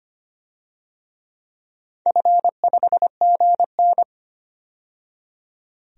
Morse code, heard as a keyed tone, about 25 wpm, 700 Hz. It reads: F5GN
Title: DXCC-100 Call Signs - 4 Characters - 25wpm